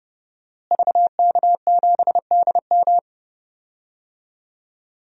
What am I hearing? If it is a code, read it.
Morse code, heard as a keyed tone, about 30 wpm, 700 Hz. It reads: VK7DM
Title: DXCC-100 Call Signs - 5 Characters - Repeated 2x - 30wpm